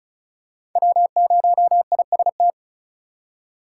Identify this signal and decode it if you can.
Morse code, heard as a keyed tone, about 36 wpm, 700 Hz. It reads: W0IST
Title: USA Call Signs - 5 Characters - 4x Character Spacing - 35wpm